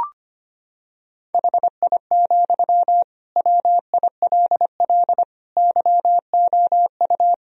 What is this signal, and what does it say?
Morse code, heard as a keyed tone, about 25 wpm, 700 Hz. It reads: HI, WILL YOU
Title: Sentences from Top 1000 Words - Review Part 3 - 25wpm